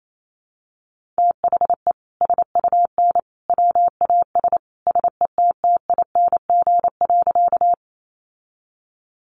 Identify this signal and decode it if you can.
Morse code, heard as a keyed tone, about 28 wpm, 700 Hz. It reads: THE SUN WAS SETTING.
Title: Sentences from Top 1300 Words 28wpm